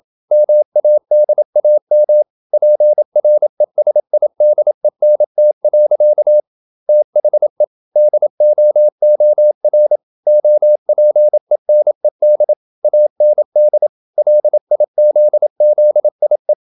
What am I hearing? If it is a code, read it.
Morse code, heard as a keyed tone, about 27 wpm, 600 Hz. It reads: MADAM PRESIDENT. THE DOOR OPENED AND LIZZIE
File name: Never 0027wpm